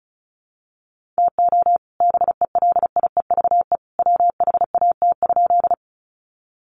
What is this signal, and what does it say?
Morse code, heard as a keyed tone, about 34 wpm, 700 Hz. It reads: TO BELIEVE WHAT?
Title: Sentences from Top 1000 Words 35wpm